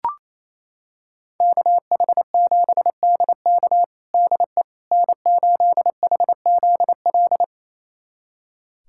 Morse code, heard as a keyed tone, about 28 wpm, 700 Hz. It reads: KH7DK DE N8HZL
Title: USA Callsign DE Callsign - 5 Characters - 28wpm